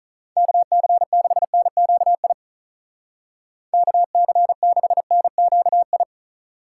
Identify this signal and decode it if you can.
Morse code, heard as a keyed tone, about 43 wpm, 700 Hz. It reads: KC6NQI KC6NQI
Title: Speed Racing - USA 2x3 Call Signs 35wpm